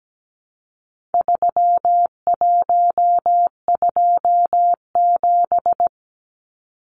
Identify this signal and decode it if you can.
Morse code, heard as a keyed tone, about 17 wpm, 700 Hz. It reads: 3127
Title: Numbers - 4-Digit 17wpm